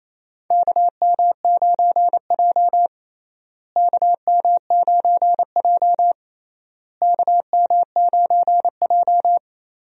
Morse code, heard as a keyed tone, about 28 wpm, 700 Hz. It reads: KM9J KM9J KM9J
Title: USA Call Signs - 4 Characters - Repeated 3x - 28wpm